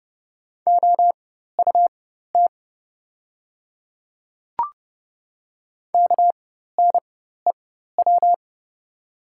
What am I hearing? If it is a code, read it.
Morse code, heard as a keyed tone, about 30 wpm, spaced as for 14 wpm, 700 Hz. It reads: OUT KNEW
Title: 4x Character Spacing - Top 500 Words - 30wpm